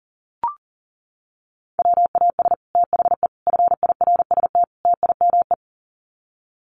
Morse code, heard as a keyed tone, about 40 wpm, 700 Hz. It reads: WAS THE FIRST TIME
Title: Sets of 4 Words 40wpm